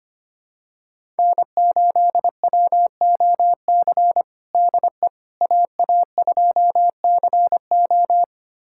Morse code, heard as a keyed tone, about 25 wpm, 700 Hz. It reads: N8WOC DE AA2CO
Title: USA Callsign DE Callsign - 5 Characters - 25wpm